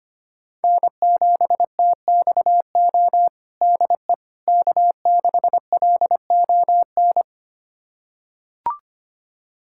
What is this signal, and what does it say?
Morse code, heard as a keyed tone, about 25 wpm, 700 Hz. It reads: N7TXO DE K6LON
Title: USA Callsign DE Callsign - 5 Characters - 25wpm